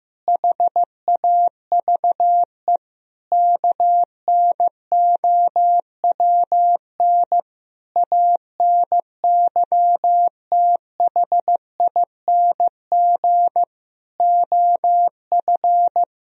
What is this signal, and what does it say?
Morse code, heard as a keyed tone, about 16 wpm, 700 Hz. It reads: HAVE KNOWN ANYTHING OF
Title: Sentences from Top 1000 Words - Review Part 3 - 15wpm